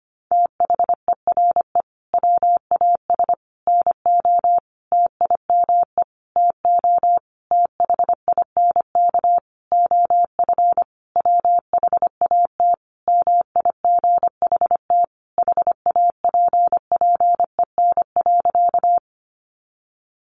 Morse code, heard as a keyed tone, about 25 wpm, 700 Hz. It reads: THERE WAS NO TIME TO THINK OF WHAT MIGHT HAPPEN.
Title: Sentences from Top 400 Words 25wpm